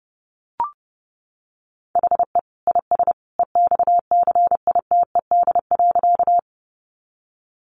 Morse code, heard as a keyed tone, about 30 wpm, 700 Hz. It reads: HE IS EXCITED.
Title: Additional Easy Phrases 30wpm